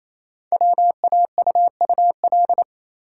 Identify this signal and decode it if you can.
Morse code, heard as a keyed tone, about 28 wpm, 700 Hz. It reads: WAUUL
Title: Code Groups - 5 Letters - 28wpm